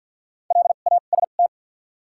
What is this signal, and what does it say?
Morse code, heard as a keyed tone, about 53 wpm, 700 Hz. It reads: L A S T
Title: 2x Character Spacing - Top 300 Words - 50wpm